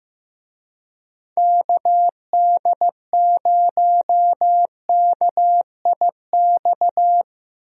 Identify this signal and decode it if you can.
Morse code, heard as a keyed tone, about 15 wpm, 700 Hz. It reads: KD0KIX
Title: USA Call Signs - 6 Characters - Repeated 3x - 15wpm